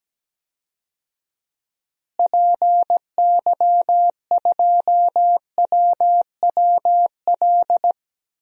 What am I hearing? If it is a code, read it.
Morse code, heard as a keyed tone, about 17 wpm, 700 Hz. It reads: PY2WWL
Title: DXCC-100 Call Signs - 6 Characters - Repeated 2x - 17wpm